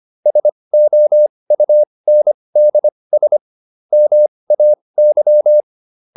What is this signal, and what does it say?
Morse code, heard as a keyed tone, about 25 wpm, spaced as for 20 wpm, 600 Hz. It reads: SOUNDS MAY